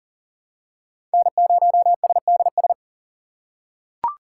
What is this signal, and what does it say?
Morse code, heard as a keyed tone, about 40 wpm, 700 Hz. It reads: N0SDS
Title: USA Call Signs - 5 Characters - 2x Character Spacing - 40wpm